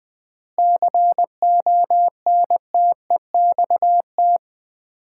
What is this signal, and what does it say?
Morse code, heard as a keyed tone, about 20 wpm, 700 Hz. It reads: CONTEXT
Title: 4x Word-Spacing - Sets of 4 Words - 20wpm